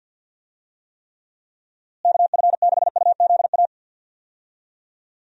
Morse code, heard as a keyed tone, about 52 wpm, 700 Hz. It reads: KF6UZA
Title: USA Call Signs - 6 Characters - Repeated 3x - 50wpm